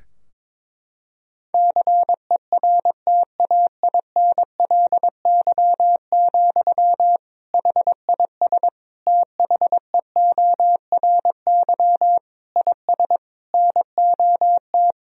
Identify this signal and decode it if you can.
Morse code, heard as a keyed tone, about 22 wpm, 700 Hz. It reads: CERTAINLY, HIS THEORY IS NOT
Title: Sentences from Top 1000 Words - Review Part 3 - 22wpm